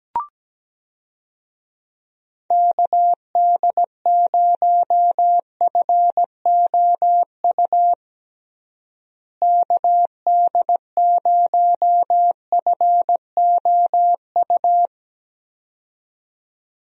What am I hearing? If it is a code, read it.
Morse code, heard as a keyed tone, about 17 wpm, 700 Hz. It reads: KD0FOU KD0FOU
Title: USA Call Signs - 6 Characters - Repeated 2x - 17wpm